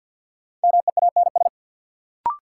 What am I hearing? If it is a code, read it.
Morse code, heard as a keyed tone, about 49 wpm, 700 Hz. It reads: MEANS
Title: Sets of 5 Words - New Word Drill - 50wpm